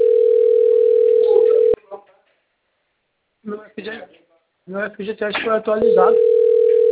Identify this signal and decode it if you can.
Ringback tone (North America), 440 + 480 Hz, with speech.
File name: out- - - - - .425